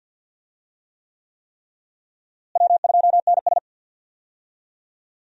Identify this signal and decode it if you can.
Morse code, heard as a keyed tone, about 46 wpm, 700 Hz. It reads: W2NS
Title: USA Call Signs - 4 Characters - Repeated 3x - 50wpm